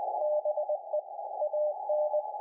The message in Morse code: BE AN